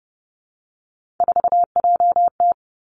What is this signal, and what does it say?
Morse code, heard as a keyed tone, about 30 wpm, 700 Hz. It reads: 4JT
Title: Three Challenging Letter-Number Combinations 30wpm